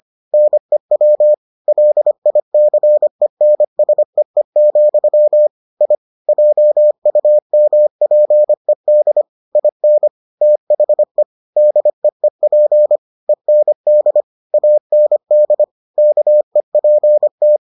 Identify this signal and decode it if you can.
Morse code, heard as a keyed tone, about 25 wpm, 600 Hz. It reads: NEW LICENSEE, I JUMPED IN THE DEEP END AND KEPT